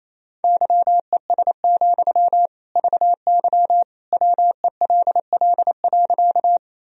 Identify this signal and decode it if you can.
Morse code, heard as a keyed tone, about 28 wpm, 700 Hz. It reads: YES, VY WELL.
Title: Sentences from Top 500 Words and Abbreviations 28wpm